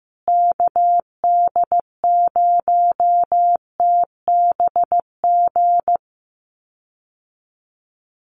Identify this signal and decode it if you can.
Morse code, heard as a keyed tone, about 15 wpm, 700 Hz. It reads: KD0TBG
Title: USA Call Signs Repeated 3x 15wpm